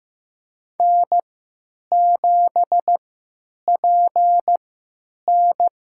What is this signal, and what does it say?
Morse code, heard as a keyed tone, about 15 wpm, spaced as for 9 wpm, 700 Hz. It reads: N7PN
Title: USA Call Signs - 4 Characters - 3x Character Spacing - 15wpm